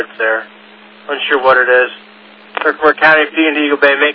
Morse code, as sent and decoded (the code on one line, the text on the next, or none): none